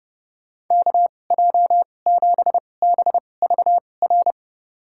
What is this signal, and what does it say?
Morse code, heard as a keyed tone, about 30 wpm, spaced as for 22 wpm, 700 Hz. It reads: KJ7BVR
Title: USA Call Signs - 6 Characters - 2x Character Spacing - 30wpm